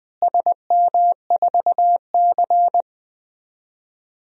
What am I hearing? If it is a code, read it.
Morse code, heard as a keyed tone, about 20 wpm, 700 Hz. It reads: SM4C
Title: DXCC-100 Call Signs - 4 Characters - Repeated 2x - 20wpm